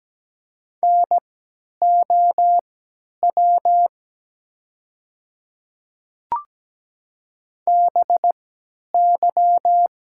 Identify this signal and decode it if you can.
Morse code, heard as a keyed tone, about 17 wpm, spaced as for 10 wpm, 700 Hz. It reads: NOW BY
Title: 3x Character Spacing - Top 100 Words - 17wpm